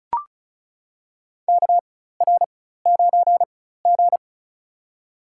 Morse code, heard as a keyed tone, about 36 wpm, spaced as for 16 wpm, 700 Hz. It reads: KR9G
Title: USA Call Signs - 4 Characters - 4x Character Spacing - 35wpm